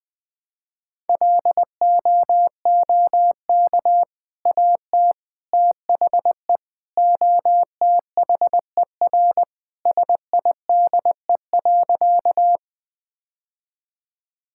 Morse code, heard as a keyed tone, about 20 wpm, 700 Hz. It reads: LOOK AT THE OTHER SIDE.
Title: Sentences with 2-Word Phrases - Spoken Beginning - 20wpm